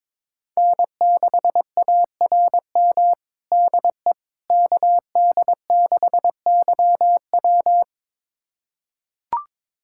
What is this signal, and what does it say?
Morse code, heard as a keyed tone, about 22 wpm, 700 Hz. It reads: N6ARM DE KD6YW
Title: USA Callsign DE Callsign - 5 Characters - 22wpm